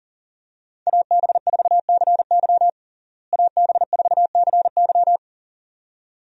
Morse code, heard as a keyed tone, about 40 wpm, 700 Hz. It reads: AB4CY AB4CY
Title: USA Call Signs - 5 Characters - Repeated 2x - 40wpm